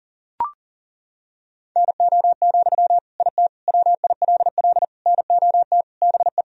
Morse code, heard as a keyed tone, about 40 wpm, 700 Hz. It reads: NO, IT WILL NOT BE
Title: Sentences from Top 1000 Words - Review Part 4 - 40wpm